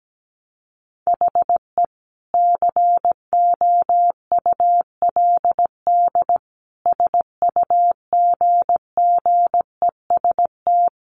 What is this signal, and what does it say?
Morse code, heard as a keyed tone, about 17 wpm, 700 Hz. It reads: HE COULD SUGGEST
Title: Sentences from Top 1300 Words 17wpm